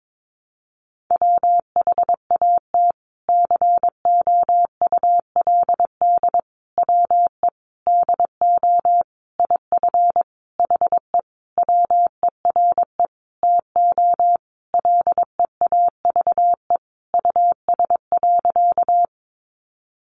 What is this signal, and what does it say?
Morse code, heard as a keyed tone, about 22 wpm, 700 Hz. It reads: WHAT COULD WE DO IF HE WERE TO LEAVE US.
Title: Sentences from Top 400 Words 22wpm